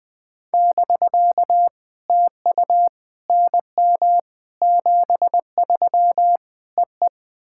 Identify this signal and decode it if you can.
Morse code, heard as a keyed tone, about 21 wpm, 700 Hz. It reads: <BK> TU NM 73 EE
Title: POTA QSO Practice - 20wpm